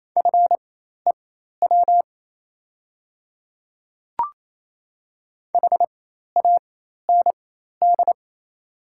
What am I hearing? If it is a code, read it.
Morse code, heard as a keyed tone, about 27 wpm, spaced as for 13 wpm, 700 Hz. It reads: FEW HAND
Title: 4x Character Spacing - Top 300 Words - 28wpm